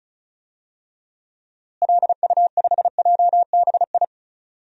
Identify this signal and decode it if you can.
Morse code, heard as a keyed tone, about 34 wpm, 700 Hz. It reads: LU5JBI
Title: DXCC-100 Call Signs - 6 Characters - Repeated 3x - 35wpm